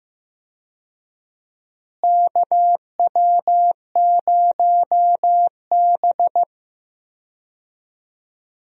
Morse code, heard as a keyed tone, about 15 wpm, 700 Hz. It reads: KW0B
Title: USA Call Signs - 4 Characters - Repeated 3x - 15wpm